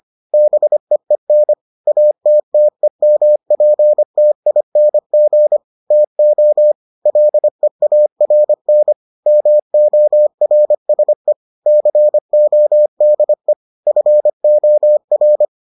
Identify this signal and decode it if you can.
Morse code, heard as a keyed tone, about 25 wpm, 600 Hz. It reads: BEEN ATTEMPTING TO LEARN MORSE CODE FOR